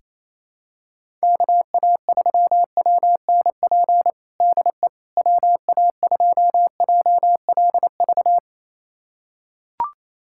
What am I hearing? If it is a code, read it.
Morse code, heard as a keyed tone, about 28 wpm, 700 Hz. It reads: KA3WNP DE WA2JLV
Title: USA Callsign DE Callsign - 6 Characters - 28wpm